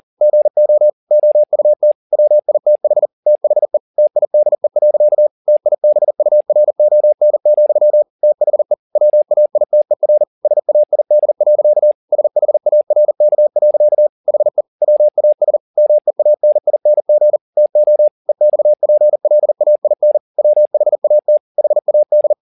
Morse code, heard as a keyed tone, about 40 wpm, 600 Hz. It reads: GO OUT WITH THE TIDE. TIBURON, THE WAITER SAID. SHARK. HE WAS MEANING TO EXPLAIN WHAT HAD
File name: the old man and the sea 2 40wpm